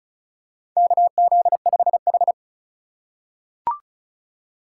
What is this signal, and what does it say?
Morse code, heard as a keyed tone, about 36 wpm, 700 Hz. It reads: KZ5H
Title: USA Call Signs - 6x Character Spacing - 35wpm